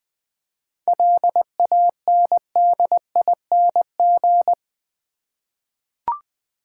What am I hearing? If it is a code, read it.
Morse code, heard as a keyed tone, about 20 wpm, 700 Hz. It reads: LANDING
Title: 7 Letter Words - First 3 Letters Given - 20wpm